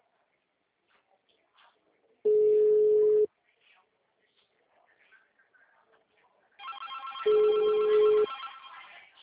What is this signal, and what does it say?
Ringback tone (Europe, 425 Hz), with speech.